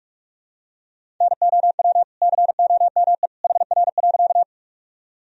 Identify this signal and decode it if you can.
Morse code, heard as a keyed tone, about 46 wpm, 700 Hz. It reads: NOW COME HR.
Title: Sentences from Top 200 Words and Abbreviations 45wpm